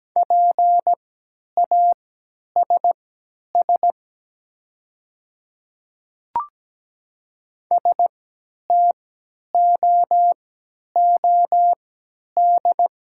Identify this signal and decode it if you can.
Morse code, heard as a keyed tone, about 17 wpm, spaced as for 10 wpm, 700 Hz. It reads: PASS STOOD
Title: 3x Character Spacing - Top 500 Words - 17wpm